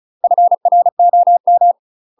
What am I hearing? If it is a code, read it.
Morse code, heard as a keyed tone, about 35 wpm, 700 Hz. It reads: FROM
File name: UK MASTER QSOs TRAINING / 35WPM / UK MASTER QSOs 0010